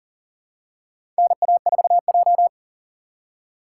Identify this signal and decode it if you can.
Morse code, heard as a keyed tone, about 40 wpm, 700 Hz. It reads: NA4J
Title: USA Call Signs - 3x Character Spacing - 40wpm